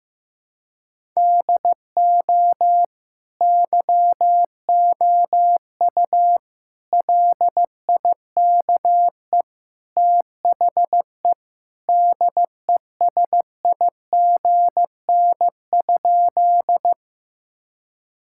Morse code, heard as a keyed tone, about 16 wpm, 700 Hz. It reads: DO YOU LIKE THE DESIGN?
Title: Sentences from Top 1000 Words - Review Part 5 - 15wpm